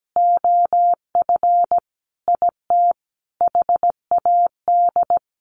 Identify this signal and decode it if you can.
Morse code, heard as a keyed tone, about 17 wpm, 700 Hz. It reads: OF IT HAD
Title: Sentences from Top 900 Words 17wpm